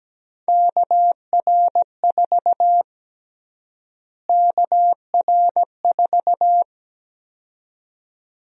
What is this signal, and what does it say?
Morse code, heard as a keyed tone, about 17 wpm, 700 Hz. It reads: KR4 KR4
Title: USA Call Sign Prefixes - Repeated 2x - 17wpm